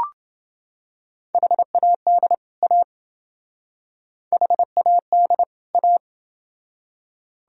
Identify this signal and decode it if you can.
Morse code, heard as a keyed tone, about 29 wpm, 700 Hz. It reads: HAD A HAD A